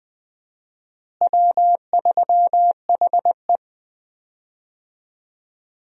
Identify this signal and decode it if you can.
Morse code, heard as a keyed tone, about 21 wpm, 700 Hz. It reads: W3HE
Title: USA Call Signs - 4 Characters - Repeated 3x - 20wpm